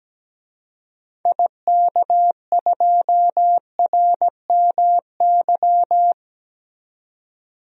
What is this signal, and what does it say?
Morse code, heard as a keyed tone, about 17 wpm, 700 Hz. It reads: IK2RMY
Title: DXCC-100 Call Signs - 6 Characters - 17wpm